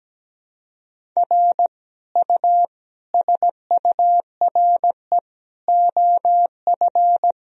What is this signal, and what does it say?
Morse code, heard as a keyed tone, about 17 wpm, 700 Hz. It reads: R U SURE OF
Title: Sentences from Top 300 Words and Abbreviations 17wpm